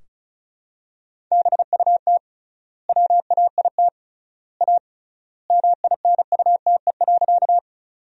Morse code, heard as a keyed tone, about 34 wpm, 700 Hz. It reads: BUT WAIT A MINUTE.
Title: 3x Word-Spacing - Sentences from Top 500 Words - 35wpm